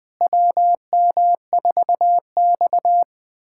Morse code, heard as a keyed tone, about 20 wpm, 700 Hz. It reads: WM4X
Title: USA Callsign DE Callsign - 4 Characters - 20wpm